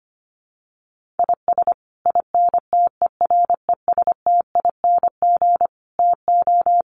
Morse code, heard as a keyed tone, about 25 wpm, 700 Hz. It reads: IS INTERESTING TO
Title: Sets of 5 Words Encore2 25wpm